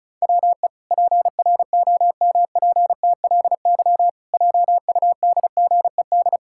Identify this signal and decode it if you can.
Morse code, heard as a keyed tone, about 36 wpm, 700 Hz. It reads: WE PROMPTLY JUDGED
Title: Pangram Challenge v2 - 35wpm